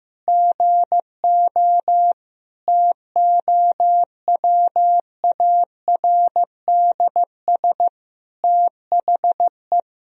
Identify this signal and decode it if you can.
Morse code, heard as a keyed tone, about 15 wpm, 700 Hz. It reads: GO TOWARDS THE
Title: Sentences from Top 1000 Words - Review Part 2 - 15wpm